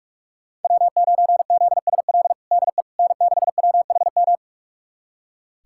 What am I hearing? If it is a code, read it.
Morse code, heard as a keyed tone, about 46 wpm, 700 Hz. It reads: W9ZSL DE N6WHM
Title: USA Callsign DE Callsign - 5 Characters - 45wpm